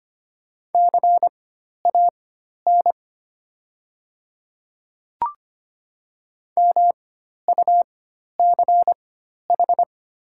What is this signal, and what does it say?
Morse code, heard as a keyed tone, about 25 wpm, spaced as for 12 wpm, 700 Hz. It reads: CAN MUCH
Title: 4x Character Spacing - Top 300 Words - 25wpm